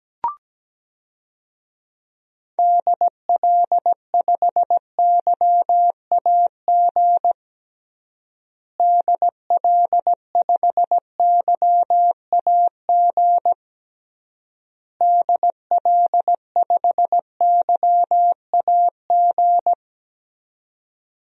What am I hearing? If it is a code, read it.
Morse code, heard as a keyed tone, about 17 wpm, 700 Hz. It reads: DL5YAG DL5YAG DL5YAG
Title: DXCC-100 Call Signs - 6 Characters - Repeated 3x - 17wpm